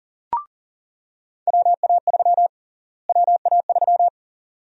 Morse code, heard as a keyed tone, about 40 wpm, 700 Hz. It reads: WA3 WA3
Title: USA Call Sign Prefixes - Repeated 2x - 40wpm